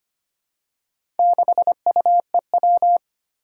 Morse code, heard as a keyed tone, about 25 wpm, 700 Hz. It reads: 6UEW
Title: Four Challenging Letter-Number Combinations 25wpm